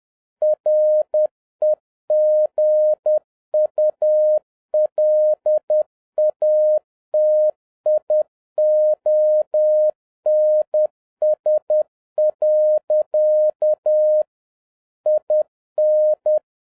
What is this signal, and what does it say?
Morse code, heard as a keyed tone, about 10 wpm, 610 Hz. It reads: REGULATIONS. IN